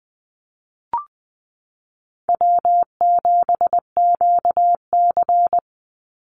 Morse code, heard as a keyed tone, about 20 wpm, 700 Hz. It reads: W7QC
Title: Easy Call Signs Repeated 2x - 20wpm